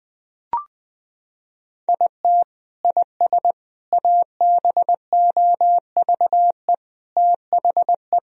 Morse code, heard as a keyed tone, about 20 wpm, 700 Hz. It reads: IT IS ABOVE THE
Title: Sentences from Top 1000 Words - Review Part 2 - 20wpm